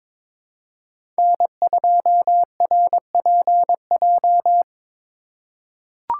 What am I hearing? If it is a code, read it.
Morse code, heard as a keyed tone, about 21 wpm, 700 Hz. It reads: N2RPJ
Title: USA Call Signs - 2x Character Spacing - 22wpm